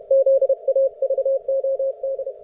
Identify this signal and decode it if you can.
Morse code, heard as a keyed tone, about 31 wpm, 540 Hz. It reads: ZAVOD